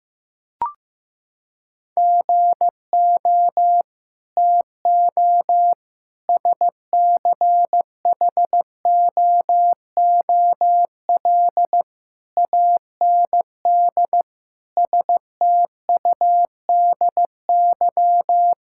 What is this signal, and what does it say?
Morse code, heard as a keyed tone, about 15 wpm, 700 Hz. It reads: GO TO SCHOOL AND STUDY
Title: Sentences from Top 1000 Words - Review Part 6 - 15wpm